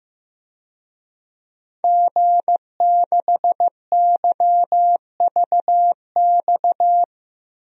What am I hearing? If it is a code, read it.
Morse code, heard as a keyed tone, about 15 wpm, 700 Hz. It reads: G6YVX
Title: DXCC-100 Call Signs - 5 Characters - Repeated 3x - 15wpm